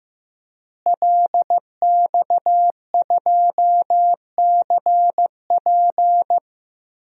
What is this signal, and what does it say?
Morse code, heard as a keyed tone, about 15 wpm, 700 Hz. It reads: LX2CP
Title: DXCC-100 Call Signs - 5 Characters - Repeated 3x - 15wpm